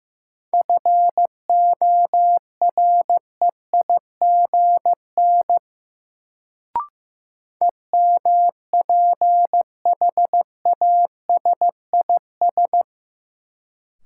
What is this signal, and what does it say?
Morse code, heard as a keyed tone, about 15 wpm, 700 Hz. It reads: FOREIGN EMPHASIS
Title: Sentences from Top 1400 Words - New Word Drill - 15wpm